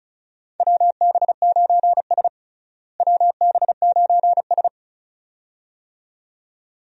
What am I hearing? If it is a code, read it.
Morse code, heard as a keyed tone, about 34 wpm, 700 Hz. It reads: WB9S WB9S
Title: USA Call Signs - 4 Characters - Repeated 2x - 35wpm